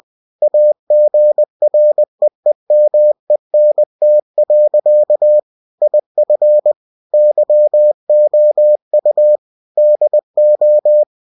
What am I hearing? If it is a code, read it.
Morse code, heard as a keyed tone, about 20 wpm, 600 Hz. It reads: AGREEMENT. IF YOU DO
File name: cw-books / winnie-the-pooh / winnie-20wpm-page-0113